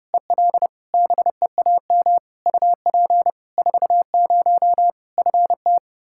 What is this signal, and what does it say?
Morse code, heard as a keyed tone, about 32 wpm, 700 Hz. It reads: EL BEAM UP 40 FT
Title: K1USN SST QSO Practice 30wpm